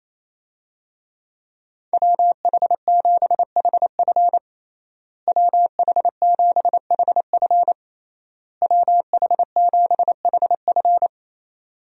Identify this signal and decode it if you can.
Morse code, heard as a keyed tone, about 28 wpm, 700 Hz. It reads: WH7HF WH7HF WH7HF
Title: USA Call Signs - 5 Characters - Repeated 3x - 28wpm